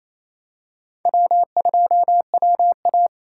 Morse code, heard as a keyed tone, about 28 wpm, 700 Hz. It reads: W2WA
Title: USA Call Signs - 4 Characters - 4x Character Spacing - 28wpm